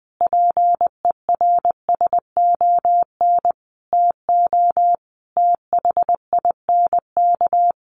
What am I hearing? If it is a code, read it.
Morse code, heard as a keyed tone, about 20 wpm, 700 Hz. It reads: PERSON TO THINK
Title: Sentences from Top 1100 Words 20wpm